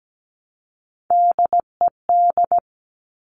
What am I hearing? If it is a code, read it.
Morse code, heard as a keyed tone, about 17 wpm, 700 Hz. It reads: DED
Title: Top Three Letter Combinations in Words 17wpm